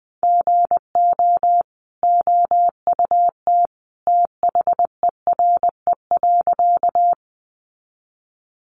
Morse code, heard as a keyed tone, about 20 wpm, 700 Hz. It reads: GO OUT THERE.
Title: A Sound Of Thunder 20wpm